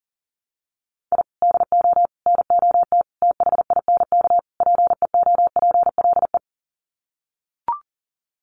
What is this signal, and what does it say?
Morse code, heard as a keyed tone, about 37 wpm, 700 Hz. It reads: I DO NOT THINK PEOPLE
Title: Sets of 5 Words Encore2 40wpm